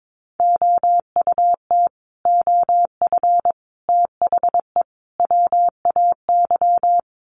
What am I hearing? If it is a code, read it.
Morse code, heard as a keyed tone, about 22 wpm, 700 Hz. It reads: OUT OF THE WAY
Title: Sets of 4 Words 22wpm